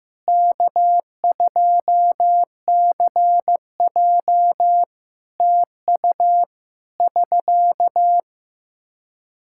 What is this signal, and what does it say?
Morse code, heard as a keyed tone, about 15 wpm, 700 Hz. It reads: K2CJ TU <SK>
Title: K1USN SST QSO Practice 15wpm